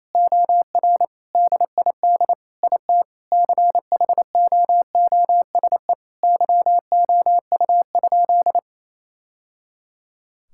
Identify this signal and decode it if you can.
Morse code, heard as a keyed tone, about 28 wpm, 700 Hz. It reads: OR DID IT CHOOSE YOU?
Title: Sentences from Top 1000 Words - Review Part 6 - 28wpm